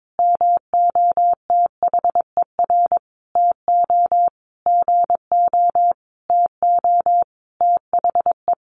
Morse code, heard as a keyed tone, about 22 wpm, 700 Hz. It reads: MOTHER TO GO TO THE
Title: Sentences from Top 1400 Words 22wpm